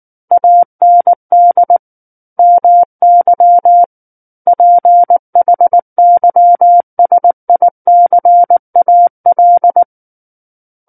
Morse code, heard as a keyed tone, about 19 wpm, 700 Hz. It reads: AND MY PHYSICAL
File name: Book One / 10-Chapter 10 - 19 WPM